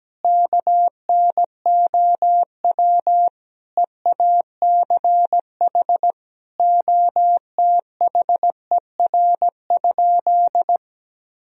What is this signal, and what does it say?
Morse code, heard as a keyed tone, about 17 wpm, 700 Hz. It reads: KNOW EACH OTHER?
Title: Sentences from Top 1000 Words - Review Part 3 - 17wpm